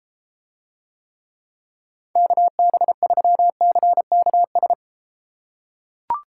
Morse code, heard as a keyed tone, about 33 wpm, 700 Hz. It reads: KB3CKS